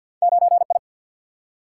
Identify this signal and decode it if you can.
Morse code, heard as a keyed tone, about 53 wpm, 700 Hz. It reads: 9 I
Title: Single-Letter Single-Number 50wpm